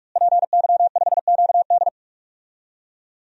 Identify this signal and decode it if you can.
Morse code, heard as a keyed tone, about 45 wpm, 700 Hz. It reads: PY5QD
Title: DXCC-100 Call Signs - 5 Characters - Repeated 3x - 45wpm